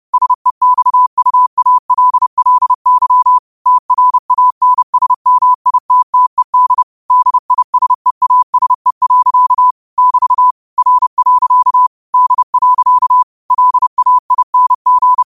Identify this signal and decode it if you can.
Morse code, heard as a keyed tone, about 30 wpm, 1000 Hz. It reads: SEXUALLY TRANSMITTED DISEASE. <BT> R. D. LAING
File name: qotdfiles / 30wpm